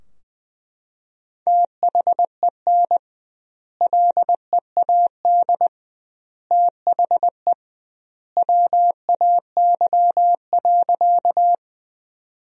Morse code, heard as a keyed tone, about 20 wpm, 700 Hz. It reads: THEN LEAD THE WAY.
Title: 2x Word-Spacing - Sentences from Top 500 Words - 20wpm